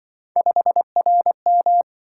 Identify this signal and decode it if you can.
Morse code, heard as a keyed tone, about 25 wpm, 700 Hz. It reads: 5RM
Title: Speed-Racing - Three Letter-Number Combinations - 20wpm